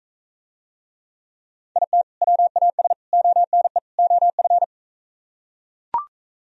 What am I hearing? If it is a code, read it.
Morse code, heard as a keyed tone, about 43 wpm, 700 Hz. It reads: IT WAS ONE OF